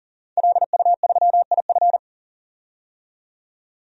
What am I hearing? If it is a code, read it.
Morse code, heard as a keyed tone, about 40 wpm, 700 Hz. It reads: LU3IF
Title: DXCC-100 Call Signs - 5 Characters - 40wpm